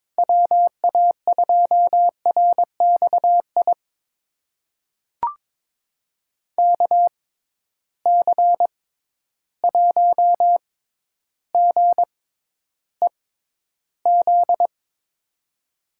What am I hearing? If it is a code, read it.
Morse code, heard as a keyed tone, about 22 wpm, spaced as for 11 wpm, 700 Hz. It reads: WA2RXI KC1GEZ
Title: USA Call Signs - 6 Characters - 6x Character Spacing - 22wpm